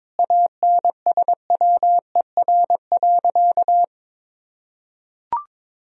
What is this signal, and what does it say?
Morse code, heard as a keyed tone, about 22 wpm, 700 Hz. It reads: ANSWER.
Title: Sentences from Top 1000 Words - Review Part 2 - 22wpm